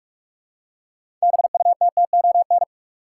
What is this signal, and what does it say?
Morse code, heard as a keyed tone, about 41 wpm, 700 Hz. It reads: BUTTON
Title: Six Letter Words - 45wpm